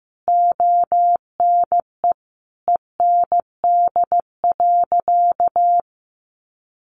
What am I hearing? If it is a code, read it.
Morse code, heard as a keyed tone, about 15 wpm, 700 Hz. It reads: ONE END.
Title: Easy Phrases 15wpm